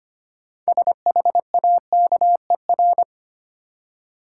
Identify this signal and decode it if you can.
Morse code, heard as a keyed tone, about 25 wpm, 700 Hz. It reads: SHAKER
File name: practice / 25wpm / Easy Words with Hints - 25wpm